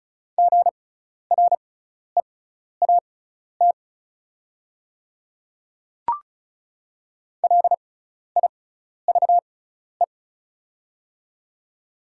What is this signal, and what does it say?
Morse code, heard as a keyed tone, about 36 wpm, spaced as for 12 wpm, 700 Hz. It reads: GREAT LIVE
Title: 6x Character Spacing - Top 300 Words - 35wpm